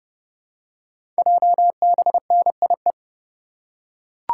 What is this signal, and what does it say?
Morse code, heard as a keyed tone, about 29 wpm, 700 Hz. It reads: JBNIE
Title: Code Groups - 5 Letters - 30wpm